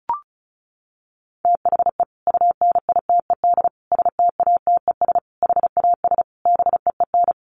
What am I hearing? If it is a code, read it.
Morse code, heard as a keyed tone, about 34 wpm, 700 Hz. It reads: THE UNITED STATES HAS BEEN
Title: Sets of 5 Words Encore2 35wpm